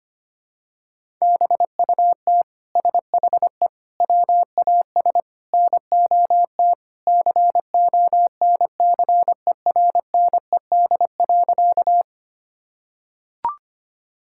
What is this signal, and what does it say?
Morse code, heard as a keyed tone, about 25 wpm, 700 Hz. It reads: BUT SHE WAS NOT CONCERNED.
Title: Sentences from Top 1000 Words - Review Part 6 - 25wpm